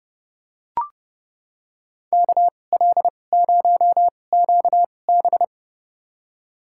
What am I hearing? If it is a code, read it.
Morse code, heard as a keyed tone, about 30 wpm, spaced as for 22 wpm, 700 Hz. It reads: KL0QB
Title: USA Call Signs - 2x Character Spacing - 30wpm